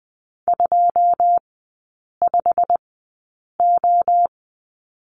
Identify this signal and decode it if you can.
Morse code, heard as a keyed tone, about 20 wpm, spaced as for 8 wpm, 700 Hz. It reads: 25O
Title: 093 Lesson 16e - Letter c - Three Characters 20wpm